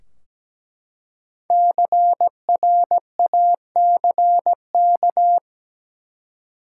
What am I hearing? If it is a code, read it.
Morse code, heard as a keyed tone, about 17 wpm, 700 Hz. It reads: CRACK
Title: Five Letter Words - 17wpm